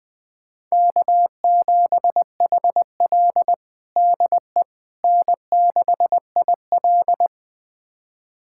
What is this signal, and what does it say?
Morse code, heard as a keyed tone, about 20 wpm, 700 Hz. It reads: K7HL DE N6IL
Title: USA Callsign DE Callsign - 4 Characters - 20wpm